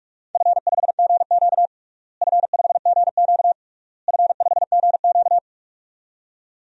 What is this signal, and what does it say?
Morse code, heard as a keyed tone, about 43 wpm, 690 Hz. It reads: F5GQ F5GQ F5GQ
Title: DXCC-100 Call Signs - 4 Characters - Repeated 3x - 45wpm